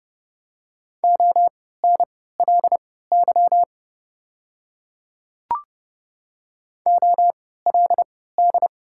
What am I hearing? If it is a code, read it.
Morse code, heard as a keyed tone, about 30 wpm, spaced as for 17 wpm, 700 Hz. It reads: ONLY OLD
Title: 3x Character Spacing - Top 200 Words - 30wpm